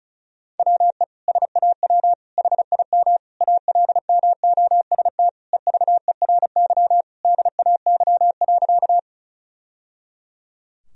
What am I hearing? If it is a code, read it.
Morse code, heard as a keyed tone, about 34 wpm, 700 Hz. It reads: WE SAW HIM ALMOST EVERY DAY.
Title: Sentences from Top 1000 Words - Review Part 4 - 35wpm